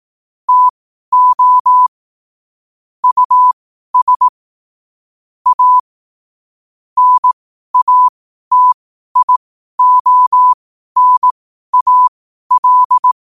Morse code, heard as a keyed tone, about 18 wpm, spaced as for 12 wpm, 1000 Hz. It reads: TO US A NATIONAL
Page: New quotes every day in morse code at 12 Words per minute.